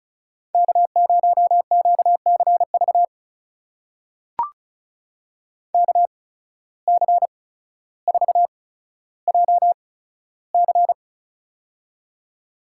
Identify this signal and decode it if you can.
Morse code, heard as a keyed tone, about 36 wpm, spaced as for 15 wpm, 700 Hz. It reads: K0QCV KC4JC
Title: USA Call Signs - 5 Characters - 8x Character Spacing - 35wpm